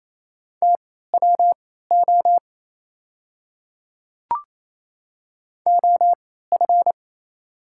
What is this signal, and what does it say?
Morse code, heard as a keyed tone, about 28 wpm, spaced as for 16 wpm, 700 Hz. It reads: TWO OF
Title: 3x Character Spacing - Top 100 Words - 28wpm